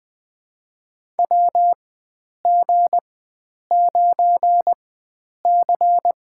Morse code, heard as a keyed tone, about 20 wpm, spaced as for 9 wpm, 700 Hz. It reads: WG9C
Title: USA Call Signs - 4 Characters - 4x Character Spacing - 20wpm